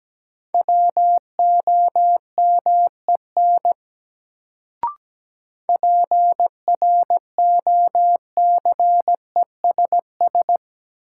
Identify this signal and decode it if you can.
Morse code, heard as a keyed tone, about 17 wpm, 700 Hz. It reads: WOMEN PROCESS
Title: Sets of 3 Words - New Word Drill - 17wpm